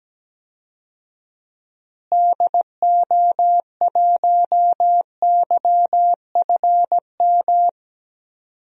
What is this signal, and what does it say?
Morse code, heard as a keyed tone, about 17 wpm, 700 Hz. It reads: DO1YFM
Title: DXCC-100 Call Signs - 6 Characters - Repeated 2x - 17wpm